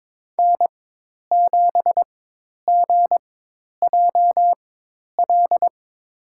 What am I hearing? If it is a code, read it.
Morse code, heard as a keyed tone, about 22 wpm, spaced as for 10 wpm, 700 Hz. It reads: N7GJL
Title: USA Call Signs - 5 Characters - 4x Character Spacing - 22wpm